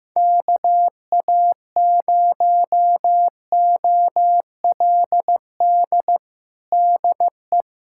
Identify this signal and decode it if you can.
Morse code, heard as a keyed tone, about 16 wpm, 700 Hz. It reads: KA0OLD DE
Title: USA Callsign DE Callsign - 6 Characters - 15wpm